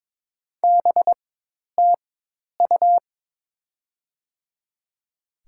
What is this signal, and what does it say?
Morse code, heard as a keyed tone, about 22 wpm, spaced as for 10 wpm, 700 Hz. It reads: BTU
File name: Top 20 QSO Elements - 4x Character Spacing - 22wpm